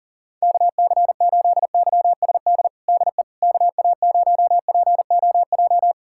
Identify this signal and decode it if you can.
Morse code, heard as a keyed tone, about 40 wpm, 700 Hz. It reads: KC8YSD DE KA0POJ
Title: USA Callsign DE Callsign - 6 Characters - 40wpm